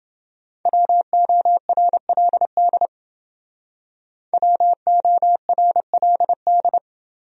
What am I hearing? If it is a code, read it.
Morse code, heard as a keyed tone, about 28 wpm, 700 Hz. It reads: WORLD WORLD